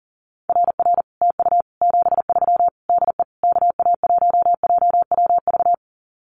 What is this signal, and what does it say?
Morse code, heard as a keyed tone, about 40 wpm, 700 Hz. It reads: RR TU 73 DE KA1JWV
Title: Common QSO Phrases Encore 40wpm